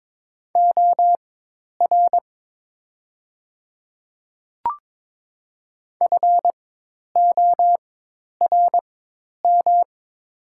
Morse code, heard as a keyed tone, about 22 wpm, spaced as for 10 wpm, 700 Hz. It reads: OR FORM
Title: 4x Character Spacing - Top 200 Words - 22wpm